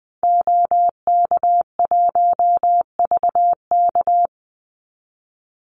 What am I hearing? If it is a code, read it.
Morse code, heard as a keyed tone, about 20 wpm, 700 Hz. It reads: OK1VK
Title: Call Signs - 20wpm